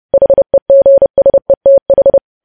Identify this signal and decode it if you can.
Morse code, heard as a keyed tone, about 30 wpm, 550 Hz. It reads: HEGSETH
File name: podcast / News headlines-1x-30x30